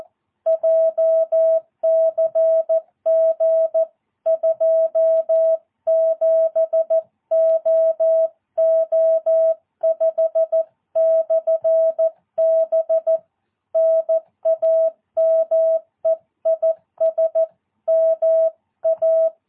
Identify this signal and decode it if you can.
Morse code, heard as a keyed tone, about 14 wpm, 650 Hz. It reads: JCG 27OO5/B NAME IS MA
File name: jh8cbh / 2022cw